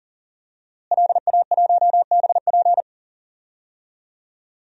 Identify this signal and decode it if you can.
Morse code, heard as a keyed tone, about 39 wpm, 700 Hz. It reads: LA1BP
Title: DXCC-100 Call Signs - 5 Characters - Repeated 3x - 40wpm